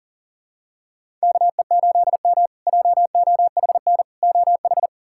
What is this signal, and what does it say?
Morse code, heard as a keyed tone, about 40 wpm, 700 Hz. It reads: KE8M JOHN OH
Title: K1USN SST QSO Practice - 40wpm